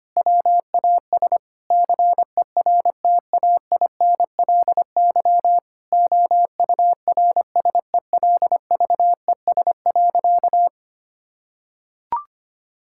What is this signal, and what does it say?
Morse code, heard as a keyed tone, about 25 wpm, 700 Hz. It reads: WAS CERTAINLY OURSELVES.
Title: Sentences from Top 1600 Words 25wpm